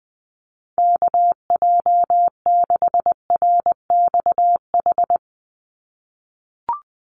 Morse code, heard as a keyed tone, about 20 wpm, 700 Hz. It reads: KJ6RXH
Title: USA Call Signs 20wpm